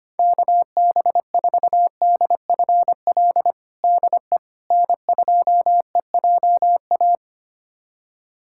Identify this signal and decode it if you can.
Morse code, heard as a keyed tone, about 25 wpm, 700 Hz. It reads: KB4DFL DE N2EJA
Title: USA Callsign DE Callsign 25wpm